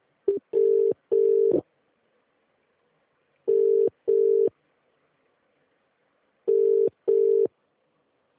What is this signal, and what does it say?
Ringback tone (UK), 400 + 450 Hz.